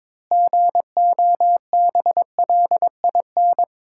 Morse code, heard as a keyed tone, about 21 wpm, 700 Hz. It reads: GOBLIN
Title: Pangram Challenge v2 - 22wpm